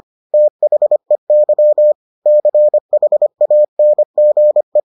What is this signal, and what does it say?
Morse code, heard as a keyed tone, about 25 wpm, 600 Hz. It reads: THEY CHANGE